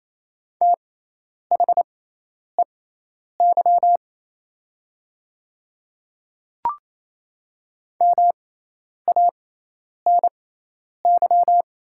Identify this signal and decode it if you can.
Morse code, heard as a keyed tone, about 28 wpm, spaced as for 10 wpm, 700 Hz. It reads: THEY MANY
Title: 6x Character Spacing - Top 100 Words - 28wpm